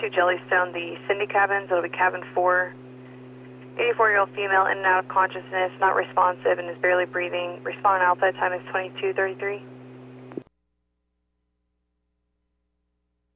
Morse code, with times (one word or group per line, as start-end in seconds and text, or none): none